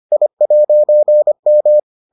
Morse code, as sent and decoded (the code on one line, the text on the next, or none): .. .----. --
I'M